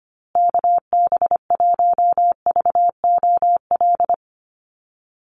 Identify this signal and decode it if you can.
Morse code, heard as a keyed tone, about 25 wpm, 700 Hz. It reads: KB1VOL
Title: USA Call Signs Repeated 3x 25wpm